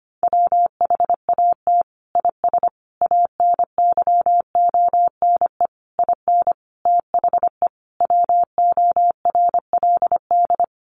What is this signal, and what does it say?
Morse code, heard as a keyed tone, about 25 wpm, 700 Hz. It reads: WHAT IS ANYONE IN THE WORLD
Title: Sentences from Top 900 Words 25wpm